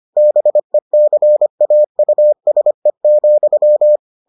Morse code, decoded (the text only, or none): BECAUSE,